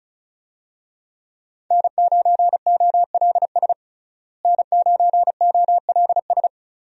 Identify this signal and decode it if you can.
Morse code, heard as a keyed tone, about 36 wpm, 700 Hz. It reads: N9OLS N9OLS
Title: USA Call Signs - 5 Characters - Repeated 2x - 35wpm